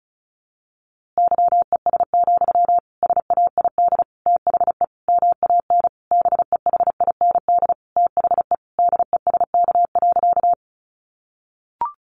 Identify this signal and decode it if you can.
Morse code, heard as a keyed tone, about 34 wpm, 700 Hz. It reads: YES, SAID THE MAN BEHIND THE DESK.
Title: A Sound Of Thunder 35wpm